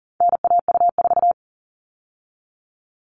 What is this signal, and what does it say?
Morse code, heard as a keyed tone, about 40 wpm, 700 Hz. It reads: NAU4
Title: All Cut Numbers - 4-Digit 40wpm